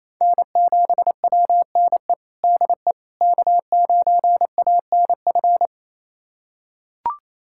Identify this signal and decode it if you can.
Morse code, heard as a keyed tone, about 28 wpm, 700 Hz. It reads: N7WNE DE K9ANF
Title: USA Callsign DE Callsign - 5 Characters - 28wpm